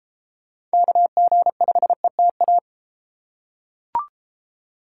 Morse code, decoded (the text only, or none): KG5ETA